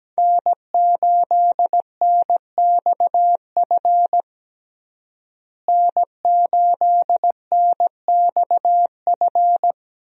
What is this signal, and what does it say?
Morse code, heard as a keyed tone, about 17 wpm, 700 Hz. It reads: N8NXF N8NXF
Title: USA Call Signs - 5 Characters - Repeated 2x - 17wpm